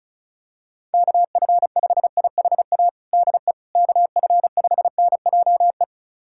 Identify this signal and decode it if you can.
Morse code, heard as a keyed tone, about 36 wpm, 700 Hz. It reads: KF5IHA DE KF5NJE
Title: USA Callsign DE Callsign 35wpm